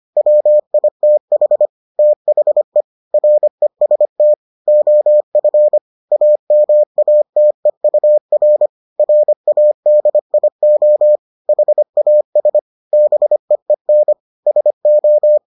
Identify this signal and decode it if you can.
Morse code, heard as a keyed tone, about 25 wpm, 600 Hz. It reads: WITH THE REST OF AMATEUR RADIO HAS BEEN SO